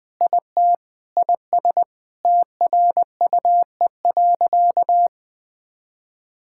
Morse code, heard as a keyed tone, about 20 wpm, 700 Hz. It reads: IT IS TRUE.
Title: Sentences with 8-Word Phrases - Spoken Beginning - 20wpm